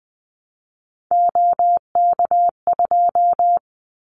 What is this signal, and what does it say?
Morse code, heard as a keyed tone, about 20 wpm, 700 Hz. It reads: OK2
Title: Callsign Prefixes for Top 25 DXCC - Repeated 3x - 20wpm